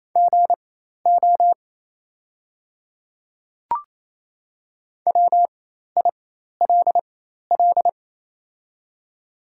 Morse code, heard as a keyed tone, about 28 wpm, spaced as for 13 wpm, 690 Hz. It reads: GO WILL
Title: 4x Character Spacing - Top 100 Words - 28wpm